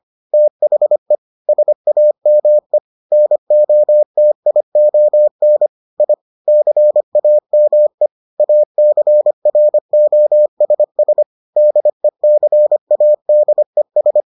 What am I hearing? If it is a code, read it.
Morse code, heard as a keyed tone, about 25 wpm, 600 Hz. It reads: THE SAME NOTION I CAME ACROSS DECADES